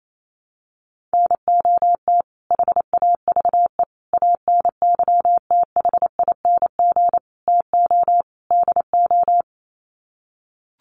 Morse code, heard as a keyed tone, about 28 wpm, 700 Hz. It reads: NOT HAVE ANYTHING TO DO
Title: Sets of 5 Words Encore 28wpm